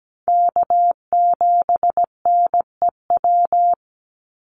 Morse code, heard as a keyed tone, about 17 wpm, 700 Hz. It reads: K7NEW
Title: NA SOTA Chaser Callsigns 17wpm